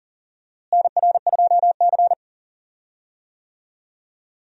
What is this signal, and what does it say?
Morse code, heard as a keyed tone, about 40 wpm, 700 Hz. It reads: NR2C
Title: USA Call Signs - 4 Characters - Repeated 3x - 40wpm